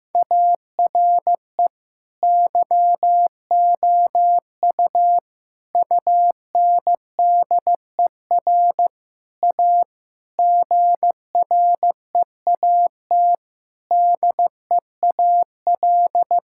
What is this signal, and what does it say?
Morse code, heard as a keyed tone, about 15 wpm, 700 Hz. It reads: ARE YOU UNDER A GREAT DEAL
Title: Sentences from Top 1000 Words - Review Part 6 - 15wpm